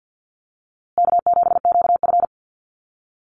Morse code, heard as a keyed tone, about 48 wpm, 700 Hz. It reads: K7QF
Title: USA Call Signs Repeated 2x 50wpm